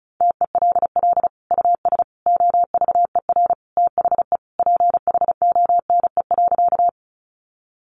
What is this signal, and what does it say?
Morse code, heard as a keyed tone, about 34 wpm, 700 Hz. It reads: TELL US OVER THE PHONE.
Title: Sentences from Top 1100 Words 35wpm